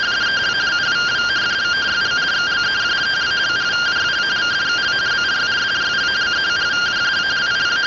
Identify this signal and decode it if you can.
Two-tone teleprinter signal, about 170 Hz shift.